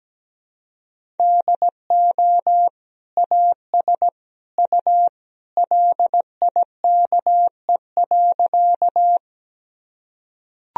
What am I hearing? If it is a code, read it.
Morse code, heard as a keyed tone, about 17 wpm, 700 Hz. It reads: DO AS U LIKE.
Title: Sentences from Top 100 Words and Abbreviations 17wpm